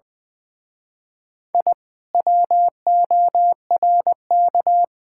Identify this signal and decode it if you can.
Morse code, heard as a keyed tone, about 20 wpm, 700 Hz. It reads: I WORK
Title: Speed-Racing Common QSO Phrases - 20wpm